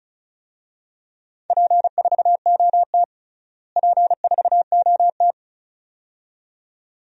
Morse code, heard as a keyed tone, about 34 wpm, 700 Hz. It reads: P4OT P4OT
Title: DXCC-100 Call Signs - 4 Characters - Repeated 2x - 35wpm